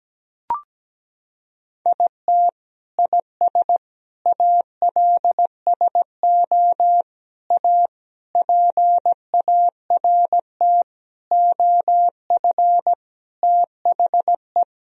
Morse code, heard as a keyed tone, about 17 wpm, 700 Hz. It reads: IT IS ALSO A PART OF THE
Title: Sentences from Top 1000 Words - Review Part 1 - 17wpm